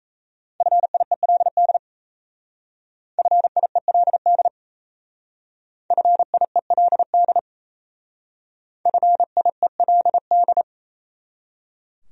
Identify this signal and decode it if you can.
Morse code, heard as a keyed tone, about 35 wpm, 700 Hz. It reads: FIELD FIELD FIELD FIELD
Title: Speed-Racing Top 500 Words 28wpm